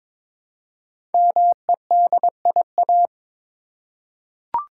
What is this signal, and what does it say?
Morse code, heard as a keyed tone, about 22 wpm, 700 Hz. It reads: MEDIA
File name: 5 Letter Words - First 3 Letters Given - 22wpm